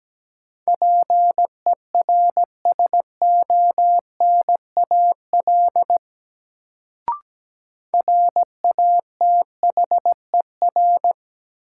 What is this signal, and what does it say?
Morse code, heard as a keyed tone, about 17 wpm, 700 Hz. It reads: PERSONAL RATHER
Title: Sentences from Top 700 Words - New Word Drill - 17wpm